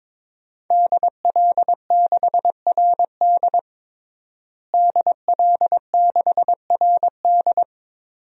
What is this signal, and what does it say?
Morse code, heard as a keyed tone, about 22 wpm, 700 Hz. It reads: DL6RD DL6RD
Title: DXCC-100 Call Signs - 5 Characters - Repeated 2x - 22wpm